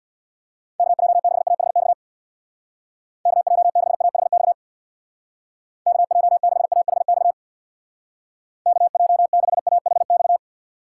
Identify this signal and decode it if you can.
Morse code, heard as a keyed tone, about 65 wpm, 700 Hz. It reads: KJ6AHX KJ6AHX KJ6AHX KJ6AHX